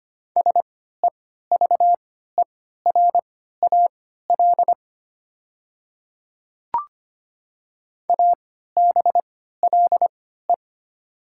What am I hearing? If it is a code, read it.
Morse code, heard as a keyed tone, about 25 wpm, spaced as for 14 wpm, 700 Hz. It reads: SEVERAL ABLE
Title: 3x Character Spacing - Top 500 Words - 25wpm